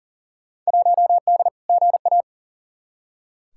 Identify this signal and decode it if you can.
Morse code, heard as a keyed tone, about 39 wpm, 700 Hz. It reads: 1D GA
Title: Field Day Exchanges - 40wpm